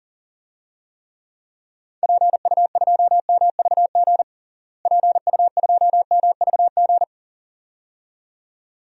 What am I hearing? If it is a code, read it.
Morse code, heard as a keyed tone, about 40 wpm, 700 Hz. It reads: PU2MVG PU2MVG
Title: DXCC-100 Call Signs - 6 Characters - Repeated 2x - 40wpm